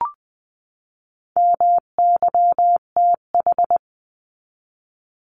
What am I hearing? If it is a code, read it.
Morse code, heard as a keyed tone, about 20 wpm, 700 Hz. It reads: MYTH
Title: 259 Lesson 42d - Words 18-20wpm